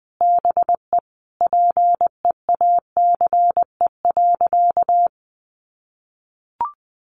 Morse code, heard as a keyed tone, about 20 wpm, 700 Hz. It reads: BE PEACE.
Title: Sentences from Top 1500 Words 20wpm